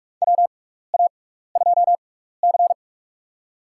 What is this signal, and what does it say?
Morse code, heard as a keyed tone, about 46 wpm, spaced as for 15 wpm, 700 Hz. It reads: WA2C
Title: USA Call Signs - 4 Characters - 6x Character Spacing - 45wpm